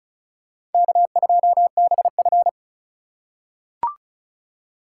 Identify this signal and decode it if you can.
Morse code, heard as a keyed tone, about 34 wpm, 700 Hz. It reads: K2BF
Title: USA Call Signs - 4 Characters - 4x Character Spacing - 35wpm